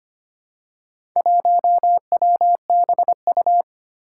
Morse code, heard as a keyed tone, about 25 wpm, 700 Hz. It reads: 1WBU
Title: Four Challenging Letter-Number Combinations 25wpm